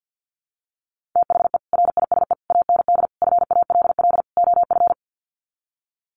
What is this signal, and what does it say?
Morse code, heard as a keyed tone, about 50 wpm, 700 Hz. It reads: THE RISE AND FALL OF
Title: Sets of 5 Words Encore2 50wpm